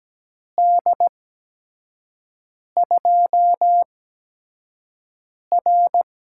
Morse code, heard as a keyed tone, about 17 wpm, spaced as for 5 wpm, 700 Hz. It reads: D2R
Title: USA Call Signs - 6 Characters - 8x Character Spacing - 17wpm